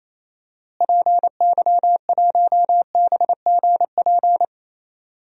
Morse code, heard as a keyed tone, about 28 wpm, 700 Hz. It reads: PY1BGP
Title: DXCC-100 Call Signs - 6 Characters - Repeated 3x - 28wpm